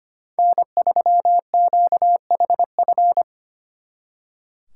Morse code, heard as a keyed tone, about 30 wpm, 700 Hz. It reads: N3QHF
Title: Speed Racing - USA 1x3 Call Signs 25wpm